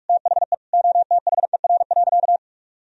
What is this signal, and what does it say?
Morse code, heard as a keyed tone, about 46 wpm, 700 Hz. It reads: THE OTHER.
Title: Sentences from Top 1000 Words - Review Part 2 - 45wpm